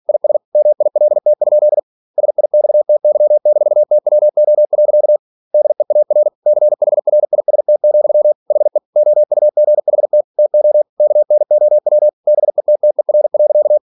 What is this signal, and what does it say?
Morse code, heard as a keyed tone, about 46 wpm, 600 Hz. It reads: IS MILT? SIXTY-TWO. DEAR CHRIST, HE OUGHT TO KNOW BETTER.
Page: Never 47wpm